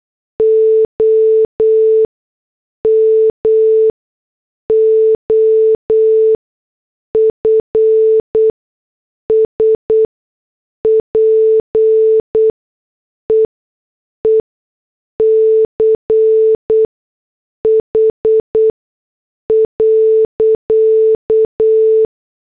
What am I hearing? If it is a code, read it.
Morse code, heard as a keyed tone, about 8 wpm, spaced as for 6 wpm, 440 Hz. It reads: OM OF SPEECH.